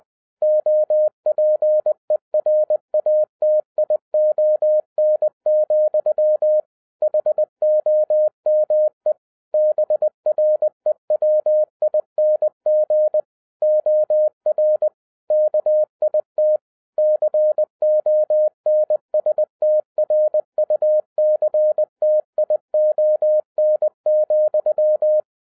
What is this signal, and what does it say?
Morse code, heard as a keyed tone, about 20 wpm, 610 Hz. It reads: OPERATION, HOME BREWING OR KIT CONSTRUCTION,
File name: 2011-10-mp3 / n13-TIPS FROM MY 34 YEARS OF EXPERIENCE 20wpm